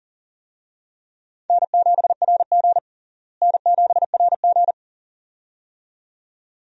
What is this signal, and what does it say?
Morse code, heard as a keyed tone, about 40 wpm, 700 Hz. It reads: N7RG N7RG
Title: USA Call Signs - 4 Characters - Repeated 2x - 40wpm